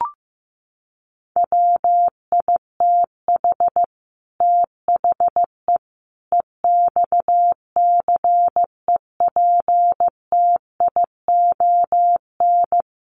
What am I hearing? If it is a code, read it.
Morse code, heard as a keyed tone, about 15 wpm, 700 Hz. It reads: WITH THE EXCEPTION
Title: Sets of 5 Words Encore 15wpm